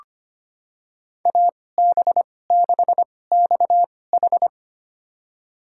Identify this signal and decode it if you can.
Morse code, heard as a keyed tone, about 25 wpm, spaced as for 18 wpm, 700 Hz. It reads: AB6XH
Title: USA Call Signs - 5 Characters - 2x Character Spacing - 25wpm